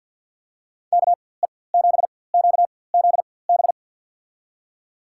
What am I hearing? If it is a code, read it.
Morse code, heard as a keyed tone, about 52 wpm, spaced as for 23 wpm, 700 Hz. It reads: KE7QZB
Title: USA Call Signs - 6 Characters - 4x Character Spacing - 50wpm